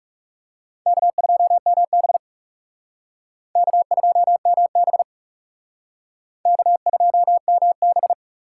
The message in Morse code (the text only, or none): K2MB K2MB K2MB